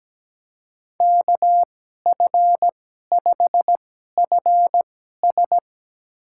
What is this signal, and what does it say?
Morse code, heard as a keyed tone, about 17 wpm, spaced as for 12 wpm, 700 Hz. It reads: KF5FS
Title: USA Call Signs - 5 Characters - 2x Character Spacing - 17wpm